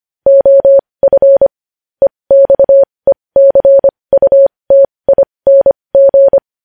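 Morse code, heard as a keyed tone, about 25 wpm, spaced as for 20 wpm, 550 Hz. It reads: OF EXECUTING
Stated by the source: Morse practice sending news headlines at 20 wpm.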